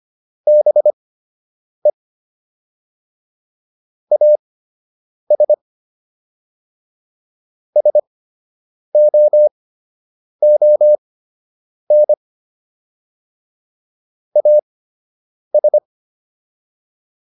Morse code, heard as a keyed tone, about 25 wpm, spaced as for 8 wpm, 600 Hz. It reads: BE AS SOON AS